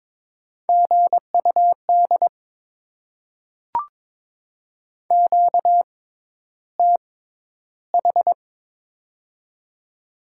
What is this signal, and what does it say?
Morse code, heard as a keyed tone, about 22 wpm, spaced as for 11 wpm, 690 Hz. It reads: GUD QTH
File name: Top 20 QSO Elements - 6x Character Spacing - 22wpm